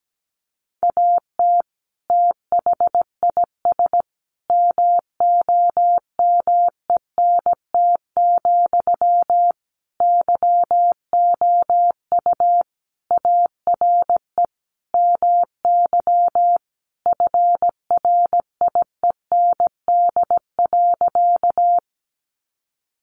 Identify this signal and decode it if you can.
Morse code, heard as a keyed tone, about 17 wpm, 700 Hz. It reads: AT THIS MOMENT, YOU ARE MY FRIEND.
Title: Sentences from Top 1000 Words 17wpm